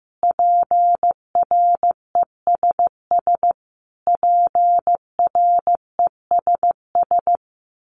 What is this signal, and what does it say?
Morse code, heard as a keyed tone, about 15 wpm, 700 Hz. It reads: PRESS PRESS
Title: Top 500 Words Repeated 15wpm